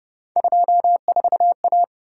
Morse code, heard as a keyed tone, about 30 wpm, 700 Hz. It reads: 24A